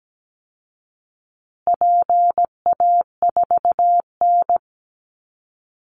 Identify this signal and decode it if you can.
Morse code, heard as a keyed tone, about 17 wpm, 700 Hz. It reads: PA4N
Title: Call Signs Repeated 2x - 17wpm